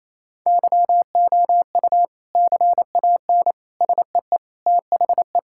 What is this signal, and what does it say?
Morse code, heard as a keyed tone, about 28 wpm, 700 Hz. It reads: YOU CAN SEE THE
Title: Sentences from Top 1000 Words - Review Part 5 - 28wpm